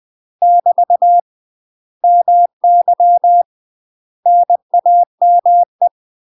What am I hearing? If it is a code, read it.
Morse code, heard as a keyed tone, about 20 wpm, 700 Hz. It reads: <BT> MY NAME
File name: UK MASTER QSOs TRAINING/20WPM